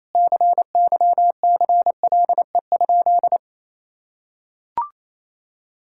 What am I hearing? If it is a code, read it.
Morse code, heard as a keyed tone, about 27 wpm, 700 Hz. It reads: CYCLE?
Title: Sentences from Top 1000 Words - Review Part 6 - 28wpm